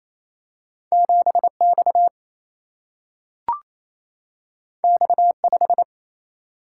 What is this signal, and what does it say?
Morse code, heard as a keyed tone, about 28 wpm, 700 Hz. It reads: 7X X5
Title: Single-Letter Single-Number 28wpm